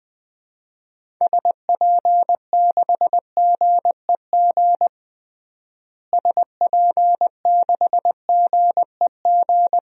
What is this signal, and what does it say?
Morse code, heard as a keyed tone, about 21 wpm, 700 Hz. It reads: SP6GEG SP6GEG
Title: DXCC-100 Call Signs - 6 Characters - Repeated 3x - 20wpm